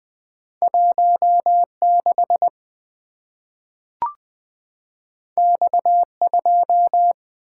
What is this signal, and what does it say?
Morse code, heard as a keyed tone, about 20 wpm, 700 Hz. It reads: 16 X2
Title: Two Letter-Number Combinations - 20wpm